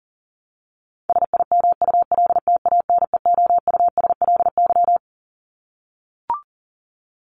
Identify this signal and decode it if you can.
Morse code, heard as a keyed tone, about 40 wpm, 700 Hz. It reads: SIMULTANEOUSLY
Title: LY Suffix - Top 100 Words with Base Words 40wpm